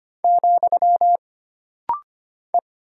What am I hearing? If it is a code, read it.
Morse code, heard as a keyed tone, about 25 wpm, 700 Hz. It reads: , E
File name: Single Character (Letter:Number:Punctuation:Prosign) - Rapid-Fire - 25wpm